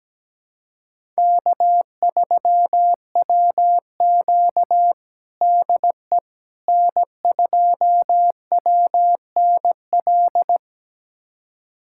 Morse code, heard as a keyed tone, about 17 wpm, 700 Hz. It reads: K3WQ DE N2WNL
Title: USA Callsign DE Callsign 17wpm